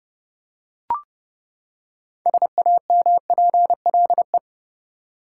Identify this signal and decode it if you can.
Morse code, heard as a keyed tone, about 30 wpm, 700 Hz. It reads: SAMPLE
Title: Six Letter Words - 30wpm